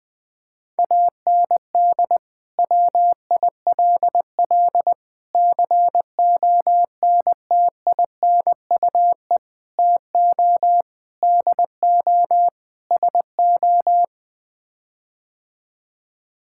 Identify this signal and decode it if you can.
Morse code, heard as a keyed tone, about 20 wpm, 700 Hz. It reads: AND WILL CONTINUE TO DO SO
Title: Sets of 6 Words - 20wpm